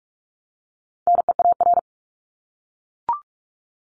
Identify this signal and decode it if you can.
Morse code, heard as a keyed tone, about 44 wpm, 700 Hz. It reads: NEAR
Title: Top 200 Words Repeated 45wpm